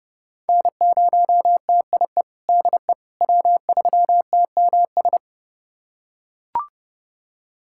Morse code, heard as a keyed tone, about 32 wpm, 700 Hz. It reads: N0TIE DE W3TMS
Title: USA Callsign DE Callsign - 5 Characters - 30wpm